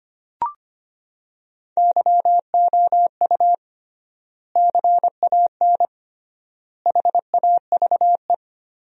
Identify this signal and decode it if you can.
Morse code, heard as a keyed tone, about 25 wpm, 700 Hz. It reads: YOU CAN HAVE
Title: 3x Word-Spacing - Sentences from Top 100 Words - 25wpm